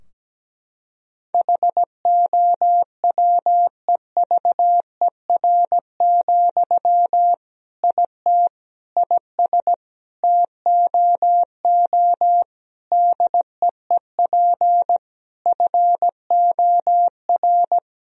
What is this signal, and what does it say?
Morse code, heard as a keyed tone, about 17 wpm, 700 Hz. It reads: HOWEVER, IT IS TOO DEEP FOR
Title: Sentences from Top 1000 Words - Review Part 3 - 17wpm